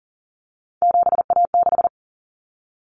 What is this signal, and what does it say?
Morse code, heard as a keyed tone, about 40 wpm, 700 Hz. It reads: 7A6
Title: Common Cut Numbers - 3-Digit 40wpm